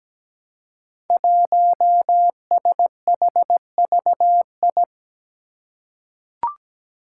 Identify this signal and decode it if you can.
Morse code, heard as a keyed tone, about 17 wpm, 700 Hz. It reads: 1SHVI
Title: Five Challenging Letter-Number Combinations 17wpm